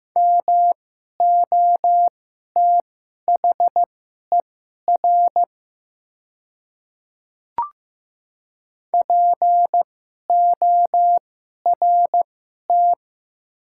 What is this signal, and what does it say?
Morse code, heard as a keyed tone, about 15 wpm, spaced as for 11 wpm, 700 Hz. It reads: MOTHER PORT
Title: 2x Character Spacing - Top 500 Words - 15wpm